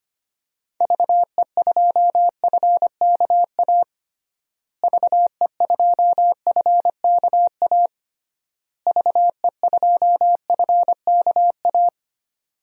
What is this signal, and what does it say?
Morse code, heard as a keyed tone, about 25 wpm, 700 Hz. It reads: VE2FKA VE2FKA VE2FKA
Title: DXCC-100 Call Signs - 6 Characters - Repeated 3x - 25wpm